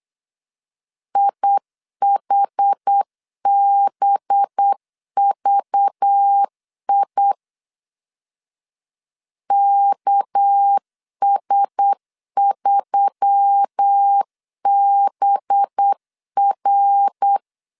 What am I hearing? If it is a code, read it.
Morse code, heard as a keyed tone, about 8 wpm, 800 Hz. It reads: IHBVI KS3BR
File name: LEZIONI / 6 PKX0 / 6 PK 8wpm 3003